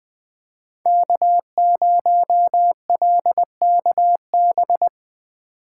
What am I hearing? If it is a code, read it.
Morse code, heard as a keyed tone, about 20 wpm, 700 Hz. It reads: K0LKB
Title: USA Call Signs - 5 Characters - 8x Character Spacing - 20wpm